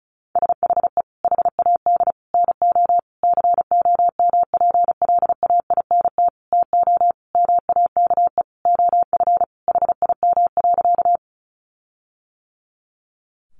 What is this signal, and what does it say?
Morse code, heard as a keyed tone, about 34 wpm, 700 Hz. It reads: SHE HAD NO COMPLAINT TO MAKE OF HIM.
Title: Sentences from Top 1400 Words 35wpm